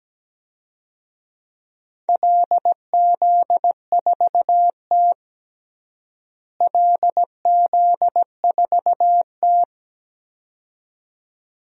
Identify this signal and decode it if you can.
Morse code, heard as a keyed tone, about 17 wpm, 700 Hz. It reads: LZ4T LZ4T
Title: DXCC-100 Call Signs - 4 Characters - Repeated 2x - 17wpm